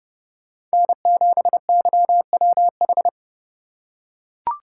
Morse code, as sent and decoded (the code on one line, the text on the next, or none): -. --... -.-- .-- ....
N7YWH